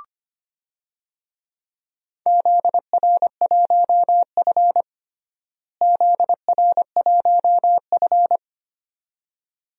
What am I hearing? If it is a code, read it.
Morse code, heard as a keyed tone, about 25 wpm, 700 Hz. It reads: ZR1F ZR1F
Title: DXCC-100 Call Signs - 4 Characters - Repeated 2x - 25wpm